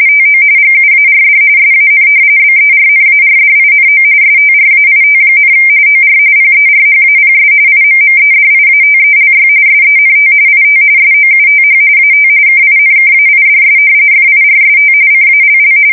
RTTY, about 45.45 baud. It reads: TO GUARD SAN BERNARDINO STRAIT, OR PERHAPS IMMEDIATELY SENT IT SOUTH AT KINKAIDS FIRST CALL
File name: fsk170 / battleshipnj